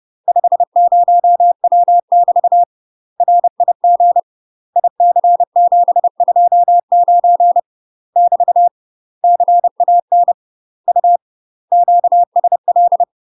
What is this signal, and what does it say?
Morse code, heard as a keyed tone, about 30 wpm, 700 Hz. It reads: 50W <BT> RIG IC729 <BT> CAN U QSL
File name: UK MASTER QSOs TRAINING/30WPM